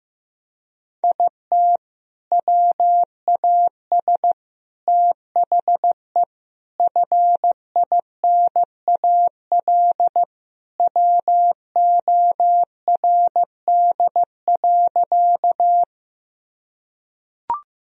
Morse code, heard as a keyed tone, about 15 wpm, 700 Hz. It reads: IT WAS THE FINAL WORD.
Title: Sentences from Top 1000 Words - Review Part 2 - 15wpm